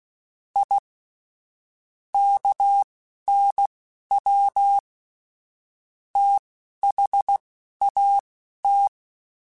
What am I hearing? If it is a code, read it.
Morse code, heard as a keyed tone, about 16 wpm, spaced as for 11 wpm, 800 Hz. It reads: I KNW THAT
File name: QSOs / QSOs-mp3s / WD0BGP 2@10